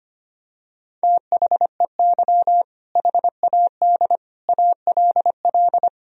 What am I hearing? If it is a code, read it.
Morse code, heard as a keyed tone, about 25 wpm, 700 Hz. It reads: THEY HAD ALL
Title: Sentences from Top 200 Words and Abbreviations 25wpm